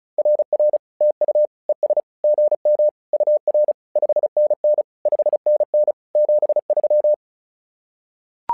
Morse code, heard as a keyed tone, about 36 wpm, 600 Hz. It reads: RR TU ES GM UR 5NN 5NN 73
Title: SOTA QSO Practice 35wpm